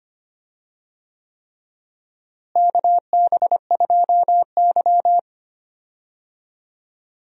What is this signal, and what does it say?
Morse code, heard as a keyed tone, about 25 wpm, 700 Hz. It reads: KB2Y
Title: USA Call Signs - 4 Characters - Repeated 3x - 25wpm